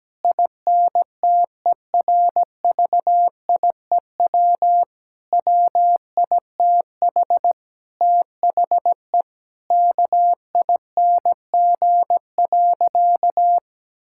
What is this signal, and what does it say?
Morse code, heard as a keyed tone, about 17 wpm, 700 Hz. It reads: INTERVIEW WITH THE KING.
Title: Sentences from Top 1600 Words 17wpm